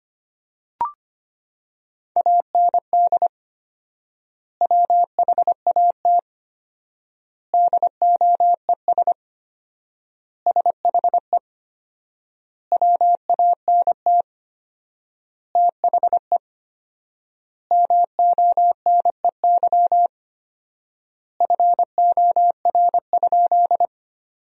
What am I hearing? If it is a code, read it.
Morse code, heard as a keyed tone, about 25 wpm, 700 Hz. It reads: AND WHAT DOES SHE WANT THE MONEY FOR?
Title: 4x Word-Spacing - Sentences from Top 400 Words - 25wpm